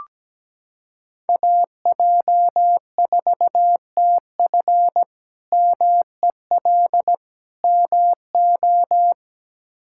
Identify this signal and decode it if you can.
Morse code, heard as a keyed tone, about 17 wpm, 700 Hz. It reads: AJ4TF MEL MO
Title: K1USN SST QSO Practice - 17wpm